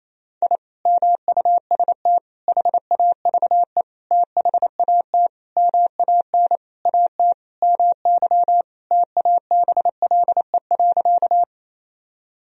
Morse code, heard as a keyed tone, about 28 wpm, 700 Hz. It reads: I MUST HAVE THAT MAN AT MY TABLE.
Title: Sentences from Top 1000 Words - Review Part 2 - 28wpm